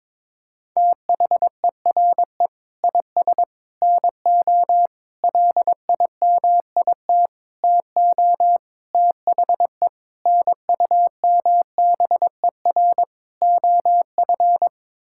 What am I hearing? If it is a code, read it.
Morse code, heard as a keyed tone, about 23 wpm, 700 Hz. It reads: THERE IS NO LIMIT TO THE NUMBER OF
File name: practice / 22wpm / Sets of 8 Words - 22wpm